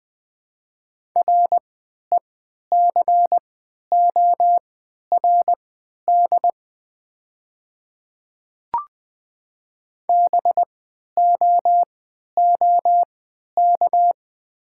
Morse code, heard as a keyed tone, about 20 wpm, spaced as for 11 wpm, 700 Hz. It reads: RECORD BOOK
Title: 3x Character Spacing - Top 500 Words - 20wpm